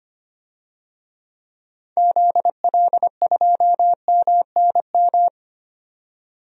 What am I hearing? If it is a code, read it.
Morse code, heard as a keyed tone, about 25 wpm, 700 Hz. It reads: ZL2MNM
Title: DXCC-100 Call Signs - 6 Characters - 25wpm